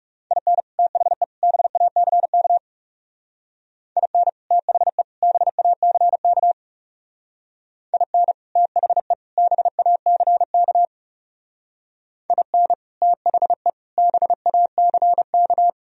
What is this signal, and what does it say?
Morse code, heard as a keyed tone, about 37 wpm, 700 Hz. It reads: IN THE BACK IN THE BACK IN THE BACK IN THE BACK
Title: Speed-Racing - Sets of 3 Words 30wpm